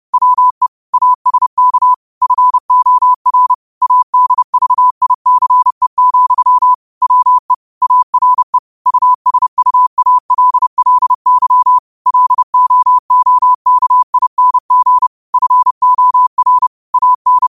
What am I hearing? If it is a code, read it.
Morse code, heard as a keyed tone, about 30 wpm, 1000 Hz. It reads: WE ASK FOR ADVICE, WE ARE USUALLY LOOKING FOR AN
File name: qotdfiles / 30wpm